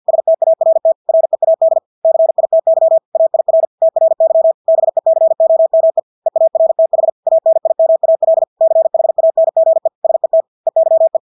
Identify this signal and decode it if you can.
Morse code, heard as a keyed tone, about 50 wpm, 650 Hz. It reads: START READ CITY AIR TRY BECOME EARTH ANIMAL CHANGE SET EYE
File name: Engelske ord 50 wpm